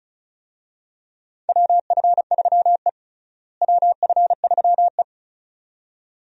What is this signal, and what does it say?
Morse code, heard as a keyed tone, about 36 wpm, 700 Hz. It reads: WF3E WF3E
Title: USA Call Signs - 4 Characters - Repeated 2x - 35wpm